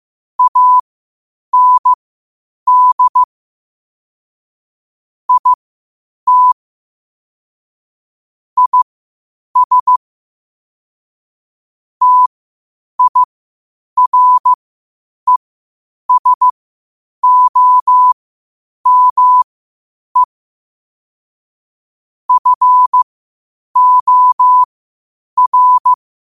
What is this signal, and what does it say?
Morse code, heard as a keyed tone, about 15 wpm, spaced as for 8 wpm, 1000 Hz. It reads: AND IT IS TIRESOME FOR
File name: qotdfiles / 8wpm